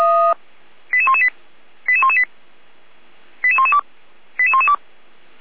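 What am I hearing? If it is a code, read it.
Five-tone selective call (ZVEI), heard as quick runs of five tones, tones of about 70 ms. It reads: 88118 88118 88112 88112
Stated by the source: ZVEI five-tone call received by SDR.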